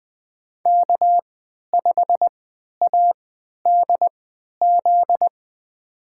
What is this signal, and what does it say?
Morse code, heard as a keyed tone, about 20 wpm, spaced as for 11 wpm, 700 Hz. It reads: K5ADZ
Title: USA Call Signs - 5 Characters - 3x Character Spacing - 20wpm